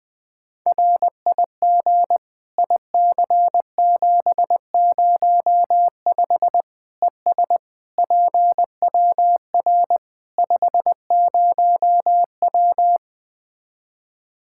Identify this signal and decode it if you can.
Morse code, heard as a keyed tone, about 21 wpm, 700 Hz. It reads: RIG IC705 ES PWR 50W
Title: K1USN SST QSO Practice 20wpm